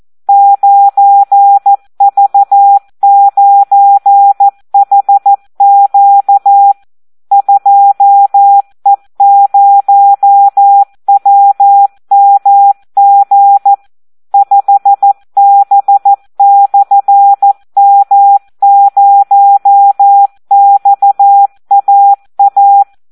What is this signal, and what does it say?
Morse code, heard as a keyed tone, about 14 wpm, 800 Hz. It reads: 9V9HQ 2E0WMG 5B/M0XAA
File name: Audio practica CW / SD 1 14